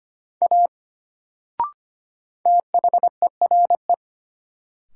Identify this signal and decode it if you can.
Morse code, heard as a keyed tone, about 25 wpm, 700 Hz. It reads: A THERE
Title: Sentences from Top 100 Words - New Word Drill - 25wpm